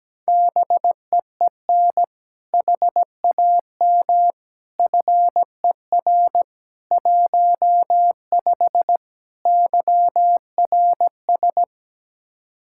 Morse code, heard as a keyed tone, about 17 wpm, 700 Hz. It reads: BEEN HAM FER 15 YRS
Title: Practice QSO - Exchange 3 - More Info - 17wpm